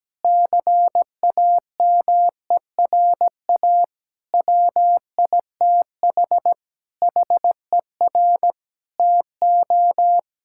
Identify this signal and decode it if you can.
Morse code, heard as a keyed tone, about 17 wpm, 700 Hz. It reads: CAMERA WITH HER TO
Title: Sentences from Top 1000 Words - Review Part 5 - 17wpm